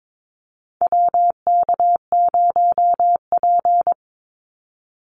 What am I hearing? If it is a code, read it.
Morse code, heard as a keyed tone, about 22 wpm, 700 Hz. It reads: WK0P
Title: Call Signs - 22wpm